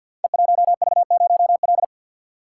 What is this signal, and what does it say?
Morse code, heard as a keyed tone, about 51 wpm, 700 Hz. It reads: E1V0L
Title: Code Groups - 5 Letter:Numbers - 50wpm